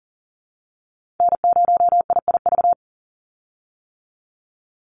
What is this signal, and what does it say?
Morse code, heard as a keyed tone, about 37 wpm, 690 Hz. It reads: N0IIV
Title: USA Call Signs 40wpm